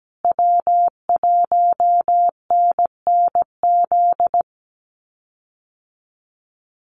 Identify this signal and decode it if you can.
Morse code, heard as a keyed tone, about 17 wpm, 700 Hz. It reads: W1NNZ
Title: USA Call Signs Repeated 3x 17wpm